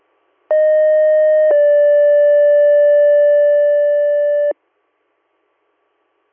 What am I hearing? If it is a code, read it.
Two-tone page, heard as a one-second tone, then a three-second tone, about 620 then 580 Hz. It is heard from 0.5 to 4.5 s.